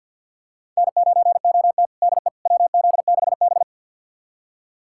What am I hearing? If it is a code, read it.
Morse code, heard as a keyed tone, about 49 wpm, 700 Hz. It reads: N9OT DE WZ6B
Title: USA Callsign DE Callsign - 4 Characters - 50wpm